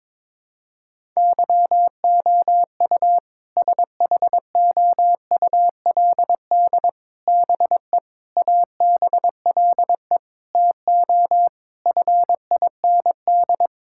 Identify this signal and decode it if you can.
Morse code, heard as a keyed tone, about 22 wpm, 700 Hz. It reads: YOU SHOULD BE ABLE TO FIND
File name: practice / 22wpm / Sentences with 4-Word Phrases - 22wpm